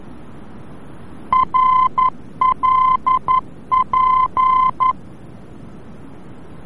Morse code, heard as a keyed tone, about 11 wpm, 1030 Hz. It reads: RLP
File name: france / mx / RLP-VOR-F